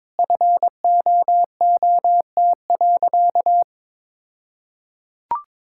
Morse code, heard as a keyed tone, about 22 wpm, 700 Hz. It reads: FOOT.
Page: morse-code-ninja-practice-22wpm